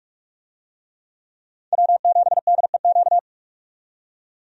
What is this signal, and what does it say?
Morse code, heard as a keyed tone, about 46 wpm, 700 Hz. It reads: W7DEQ
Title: USA Call Signs - 5 Characters - 45wpm